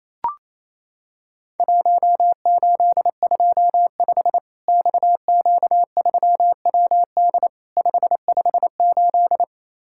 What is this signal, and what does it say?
Morse code, heard as a keyed tone, about 27 wpm, 700 Hz. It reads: 1825 XQ3WD 558
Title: Worked All Europe DX Contest - QTC Exchange - 28wpm